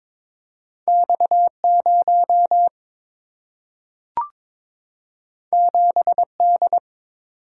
Morse code, heard as a keyed tone, about 22 wpm, 700 Hz. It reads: X0 7D
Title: Single-Letter Single-Number 22wpm